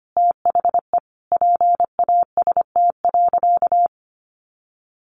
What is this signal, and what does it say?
Morse code, heard as a keyed tone, about 25 wpm, 700 Hz. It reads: THE PAST.
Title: Sentences from Top 1200 Words 25wpm